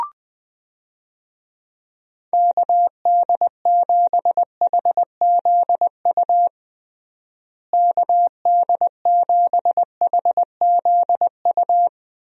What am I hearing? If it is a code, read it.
Morse code, heard as a keyed tone, about 20 wpm, 700 Hz. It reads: KD7HZU KD7HZU
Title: USA Call Signs - 6 Characters - Repeated 3x - 20wpm